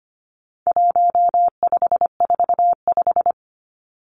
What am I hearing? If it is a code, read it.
Morse code, heard as a keyed tone, about 25 wpm, 700 Hz. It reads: 1545
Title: Numbers - 4-Digit 25wpm